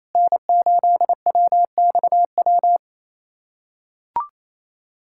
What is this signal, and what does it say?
Morse code, heard as a keyed tone, about 27 wpm, 700 Hz. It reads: N8WXW
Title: USA Callsign DE Callsign - 5 Characters - 28wpm